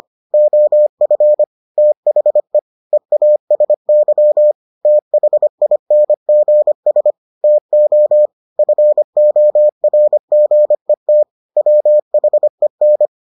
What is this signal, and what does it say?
Morse code, heard as a keyed tone, about 25 wpm, 600 Hz. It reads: OF THE EASY THINGS TO FORGET WHEN